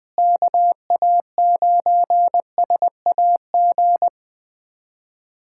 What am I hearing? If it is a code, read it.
Morse code, heard as a keyed tone, about 21 wpm, 700 Hz. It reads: KA9SAG
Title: USA Callsign DE Callsign - 6 Characters - 20wpm